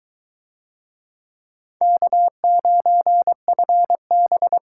Morse code, heard as a keyed tone, about 21 wpm, 700 Hz. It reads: K9FB